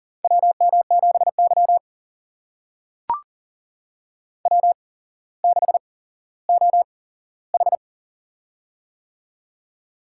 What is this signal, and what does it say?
Morse code, heard as a keyed tone, about 40 wpm, spaced as for 17 wpm, 700 Hz. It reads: WM7Y W6OH
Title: USA Call Signs - 4 Characters - 8x Character Spacing - 40wpm